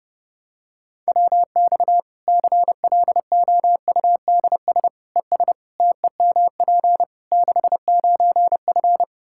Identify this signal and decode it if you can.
Morse code, heard as a keyed tone, about 29 wpm, 700 Hz. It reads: WX CLOUDS ES TEMP 69F
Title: K1USN SST QSO Practice 30wpm